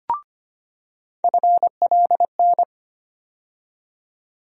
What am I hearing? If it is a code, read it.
Morse code, heard as a keyed tone, about 25 wpm, 700 Hz. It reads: FLN
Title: USA Call Sign Postfixes - 3 Letters 25wpm